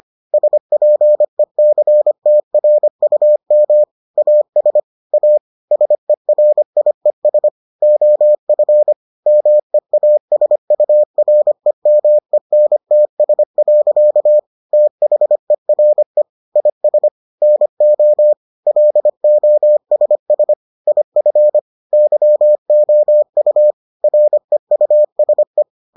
Morse code, heard as a keyed tone, about 25 wpm, 600 Hz. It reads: SPECTRUM AS A SERIES OF MEASUREMENTS. THERE IS NO LOSS IF YOU REUSE